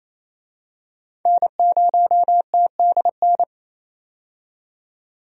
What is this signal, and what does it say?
Morse code, heard as a keyed tone, about 28 wpm, 700 Hz. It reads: N0TDN
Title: USA Call Signs - 5 Characters - 28wpm